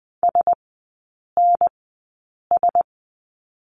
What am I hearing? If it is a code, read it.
Morse code, heard as a keyed tone, about 20 wpm, spaced as for 8 wpm, 700 Hz. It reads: SNS
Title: 035 Lesson 7e - Letter s - Three Characters 20wpm